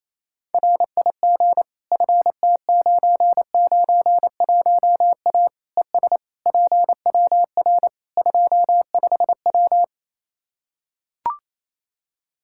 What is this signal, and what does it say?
Morse code, heard as a keyed tone, about 28 wpm, 700 Hz. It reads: RIG FT991A ES PWR 25W
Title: Practice QSO - Exchange 2 - Info - 28wpm